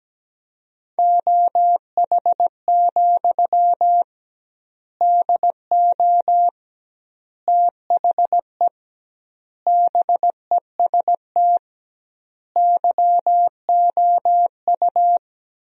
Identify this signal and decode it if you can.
Morse code, heard as a keyed tone, about 17 wpm, 700 Hz. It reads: OH, DO THE BEST YOU
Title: 2x Word-Spacing - Sentences from Top 500 Words - 17wpm